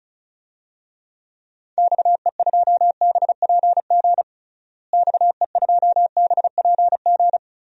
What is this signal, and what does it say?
Morse code, heard as a keyed tone, about 36 wpm, 700 Hz. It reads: XE2BPG XE2BPG
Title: DXCC-100 Call Signs - 6 Characters - Repeated 3x - 35wpm